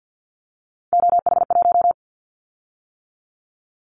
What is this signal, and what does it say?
Morse code, heard as a keyed tone, about 50 wpm, 700 Hz. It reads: OH1
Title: Callsign Prefixes for Top 50 DXCC - 50wpm